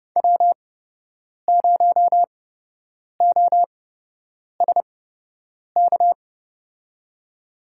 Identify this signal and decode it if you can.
Morse code, heard as a keyed tone, about 30 wpm, spaced as for 8 wpm, 700 Hz. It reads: W0OSK
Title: USA Call Signs - 8x Character Spacing - 30wpm